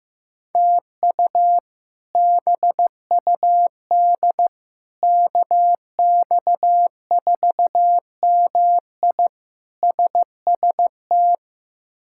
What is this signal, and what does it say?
Morse code, heard as a keyed tone, about 15 wpm, 700 Hz. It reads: TU BUD KX4MI SST
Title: K1USN SST QSO Practice - 15wpm